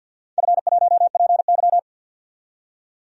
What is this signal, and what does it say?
Morse code, heard as a keyed tone, about 46 wpm, 700 Hz. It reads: F1PY
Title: DXCC-100 Call Signs - 4 Characters - 50wpm